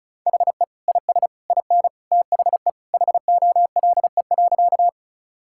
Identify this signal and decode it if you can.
Morse code, heard as a keyed tone, about 36 wpm, 700 Hz. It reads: HE IS IN THE HOLE.
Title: Sentences from Top 1600 Words 35wpm